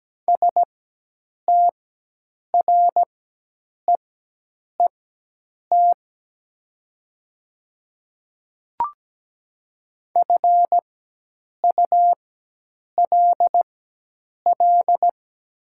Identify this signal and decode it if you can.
Morse code, heard as a keyed tone, about 17 wpm, spaced as for 8 wpm, 700 Hz. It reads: STREET FULL
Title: 4x Character Spacing - Top 500 Words - 17wpm